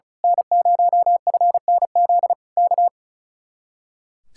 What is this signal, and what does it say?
Morse code, heard as a keyed tone, about 38 wpm, 700 Hz. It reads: N0FNZ K
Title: Speed-Racing Common QSO Phrases - 35wpm